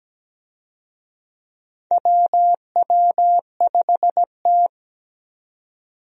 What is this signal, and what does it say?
Morse code, heard as a keyed tone, about 17 wpm, 700 Hz. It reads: WW5T
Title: USA Call Signs - 4 Characters - 17wpm